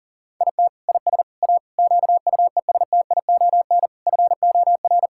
Morse code, heard as a keyed tone, about 40 wpm, 690 Hz. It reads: IT IS A QUESTION FOR
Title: Sentences from Top 1000 Words - Review Part 6 - 40wpm